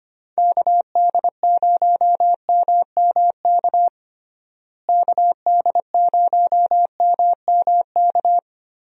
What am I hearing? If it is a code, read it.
Morse code, heard as a keyed tone, about 25 wpm, 700 Hz. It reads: KD0MMK KD0MMK
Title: USA Call Signs - 6 Characters - Repeated 3x - 25wpm